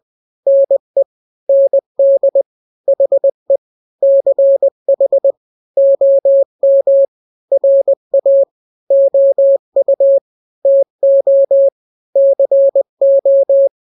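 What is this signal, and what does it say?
Morse code, heard as a keyed tone, about 20 wpm, 550 Hz. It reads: NE ND HE CH OM RA OU TO CO
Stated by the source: Morse drill of two-letter n-grams at 20/18 WPM (Farnsworth), from KK7RR.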